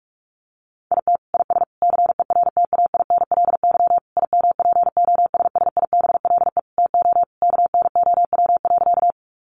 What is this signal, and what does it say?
Morse code, heard as a keyed tone, about 44 wpm, 700 Hz. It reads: IT IS CERTAINLY IMPOSSIBLE TO KNOW.
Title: Sentences from Top 1100 Words 45wpm